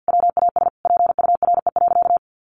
Morse code, heard as a keyed tone, about 50 wpm, 700 Hz. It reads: WAS PURE.
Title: Sentences from Top 1500 Words 50wpm